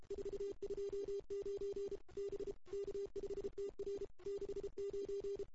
Morse code, heard as a keyed tone, about 30 wpm, 410 Hz. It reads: 429 B K5TR 69
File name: CW / 2008-ss-cw-k5tr-op-n5rz -01:00